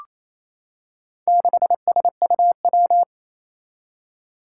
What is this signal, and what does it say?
Morse code, heard as a keyed tone, about 27 wpm, 700 Hz. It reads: 6SUW
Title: Four Challenging Letter-Number Combinations 28wpm